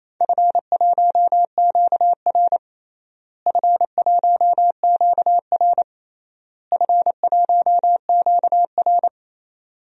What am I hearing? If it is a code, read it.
Morse code, heard as a keyed tone, about 28 wpm, 700 Hz. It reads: F1QR F1QR F1QR
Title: DXCC-100 Call Signs - 4 Characters - Repeated 3x - 28wpm